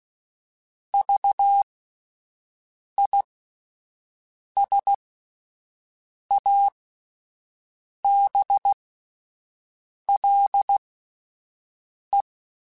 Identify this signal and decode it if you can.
Morse code, heard as a keyed tone, about 16 wpm, spaced as for 5 wpm, 800 Hz. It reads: VISABLE
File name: Alphabetical / Alphabetical mp3s / Vs @5